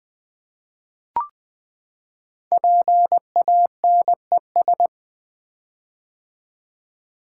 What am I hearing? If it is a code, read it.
Morse code, heard as a keyed tone, about 20 wpm, 700 Hz. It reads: PANES
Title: Code Groups - 5 Letters - 20wpm